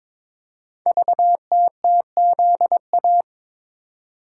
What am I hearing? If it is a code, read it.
Morse code, heard as a keyed tone, about 23 wpm, 700 Hz. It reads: VTTZA
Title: Code Groups - 5 Letters - 22wpm